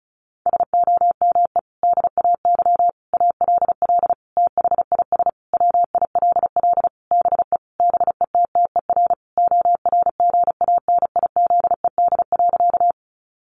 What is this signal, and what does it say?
Morse code, heard as a keyed tone, about 34 wpm, 700 Hz. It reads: SOME DAY ALL THIS WILL BE BETTER ORGANIZED.
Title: Sentences from Top 1300 Words 35wpm